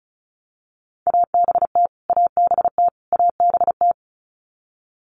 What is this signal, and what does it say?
Morse code, heard as a keyed tone, about 34 wpm, 700 Hz. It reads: ABT ABT ABT
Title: Abbreviations Repeated 3x - 35wpm